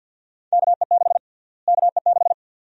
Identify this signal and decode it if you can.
Morse code, heard as a keyed tone, about 53 wpm, 700 Hz. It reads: KE6 KE6
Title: USA Call Sign Prefixes - Repeated 2x - 50wpm